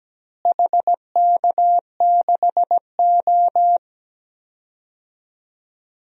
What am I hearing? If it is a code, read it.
Morse code, heard as a keyed tone, about 17 wpm, 700 Hz. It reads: HK6O
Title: DXCC-100 Call Signs - 4 Characters - Repeated 3x - 17wpm